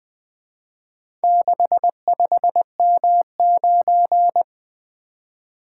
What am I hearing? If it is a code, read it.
Morse code, heard as a keyed tone, about 23 wpm, 700 Hz. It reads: 65M9
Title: Speed-Racing - Four Letter-Number Combinations - 20wpm